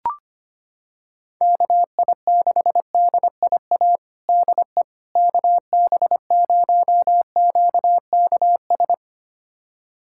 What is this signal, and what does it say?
Morse code, heard as a keyed tone, about 25 wpm, 700 Hz. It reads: KI6DIA DE KB0QKS
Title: USA Callsign DE Callsign 25wpm